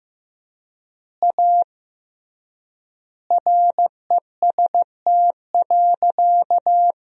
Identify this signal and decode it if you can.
Morse code, heard as a keyed tone, about 16 wpm, 700 Hz. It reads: A REST.
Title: 3x Word-Spacing - Sentences from Top 500 Words - 15wpm